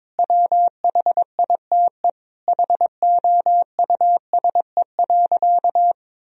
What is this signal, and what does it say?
Morse code, heard as a keyed tone, about 22 wpm, 700 Hz. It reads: WHITE HOUSE.
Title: Sentences from Top 1000 Words - Review Part 1 - 22wpm